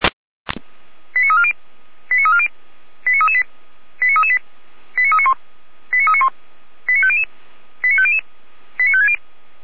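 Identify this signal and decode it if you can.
Five-tone selective call (ZVEI), heard as quick runs of five tones, tones of about 70 ms. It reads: 89340 89340 89338 89338 89391 89391 89500 89500 89560
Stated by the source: ZVEI five-tone call 89325, received by SDR.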